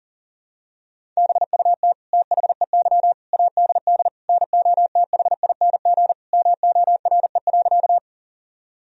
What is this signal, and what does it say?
Morse code, heard as a keyed tone, about 40 wpm, 700 Hz. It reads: BUT THEY ADD NOTHING MORE.
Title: Sentences from Top 1000 Words - Review Part 1 - 40wpm